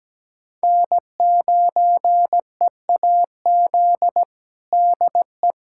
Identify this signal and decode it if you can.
Morse code, heard as a keyed tone, about 17 wpm, 700 Hz. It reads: N9EAZ DE
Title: USA Callsign DE Callsign 17wpm